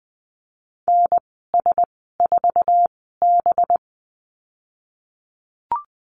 Morse code, heard as a keyed tone, about 20 wpm, spaced as for 15 wpm, 700 Hz. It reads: NS4B
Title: 193 Lesson 31g - Letter v - Call Signs 20wpm